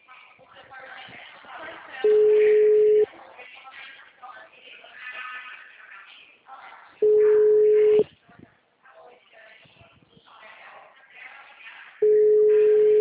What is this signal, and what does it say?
Ringback tone (Europe, 425 Hz), with speech.